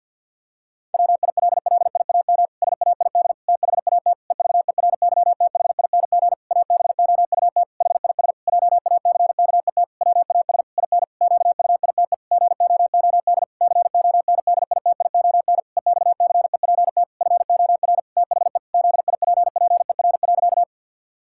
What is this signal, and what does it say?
Morse code, heard as a keyed tone, about 52 wpm, 700 Hz. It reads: WILLIAM SAID THAT EVERYTHING ABOUT HIS JACKET WAS IN QUITE GOOD CONDITION EXCEPT FOR THE ZIPPER.
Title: Pangram Challenge v2 - 50wpm